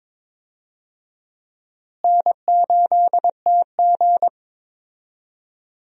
Morse code, heard as a keyed tone, about 22 wpm, 700 Hz. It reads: N8TG
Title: USA Call Signs - 4 Characters - Repeated 3x - 22wpm